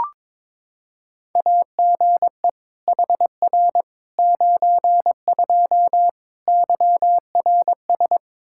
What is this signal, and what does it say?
Morse code, heard as a keyed tone, about 22 wpm, 700 Hz. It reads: AGE HR 92 YRS
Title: K1USN SST QSO Practice 22wpm